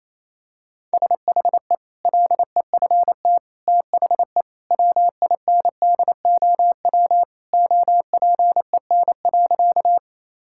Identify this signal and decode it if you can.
Morse code, heard as a keyed tone, about 28 wpm, 700 Hz. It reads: SHE LEFT THE WINDOW OPEN.
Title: Sentences from Top 1600 Words 28wpm